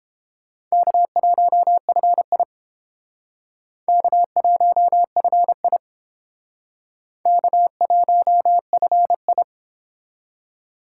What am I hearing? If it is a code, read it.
Morse code, heard as a keyed tone, about 30 wpm, 700 Hz. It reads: K1FI K1FI K1FI